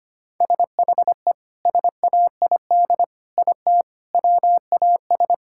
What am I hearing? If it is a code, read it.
Morse code, heard as a keyed tone, about 25 wpm, 700 Hz. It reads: SHE SAID IT WAS
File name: practice / 25wpm / Sentences with 2-Word Phrases - 25wpm